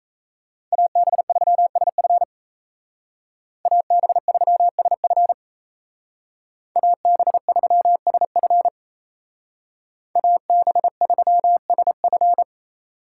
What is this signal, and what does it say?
Morse code, heard as a keyed tone, about 35 wpm, 700 Hz. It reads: AB3SF AB3SF AB3SF AB3SF
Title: Speed Racing - USA 2x2 Call Signs 28wpm